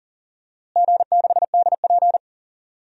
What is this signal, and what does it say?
Morse code, heard as a keyed tone, about 38 wpm, 700 Hz. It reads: G6DP
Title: DXCC-100 Call Signs - 4 Characters - Repeated 2x - 40wpm